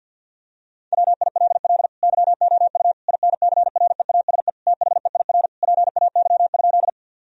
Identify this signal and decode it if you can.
Morse code, heard as a keyed tone, about 50 wpm, 700 Hz. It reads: WILL YOU INCREASE THEIR PAY?
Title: Sentences from Top 1000 Words - Review Part 4 - 50wpm